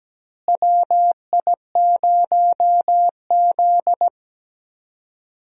Morse code, heard as a keyed tone, about 17 wpm, 700 Hz. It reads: WI0Z
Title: USA Call Signs - 4 Characters - Repeated 3x - 17wpm